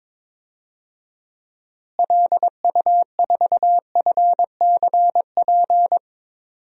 Morse code, heard as a keyed tone, about 22 wpm, 700 Hz. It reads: LU4FCP
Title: DXCC-100 Call Signs - 6 Characters - Repeated 2x - 22wpm